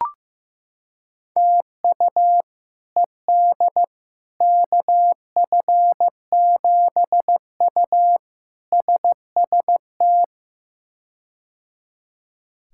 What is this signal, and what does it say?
Morse code, heard as a keyed tone, about 15 wpm, 700 Hz. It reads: TU ED KF7U SST
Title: K1USN SST QSO Practice - 15wpm